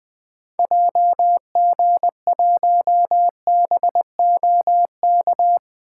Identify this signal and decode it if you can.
Morse code, heard as a keyed tone, about 20 wpm, 700 Hz. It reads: JG1BOK
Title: 2023 SOTA Activator Call Signs - 20wpm